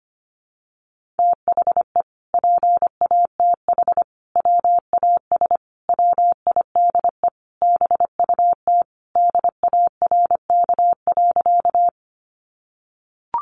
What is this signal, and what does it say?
Morse code, heard as a keyed tone, about 25 wpm, 700 Hz. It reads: THE PATH WAS WIDE BUT DARK.
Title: Sentences from Top 1400 Words 25wpm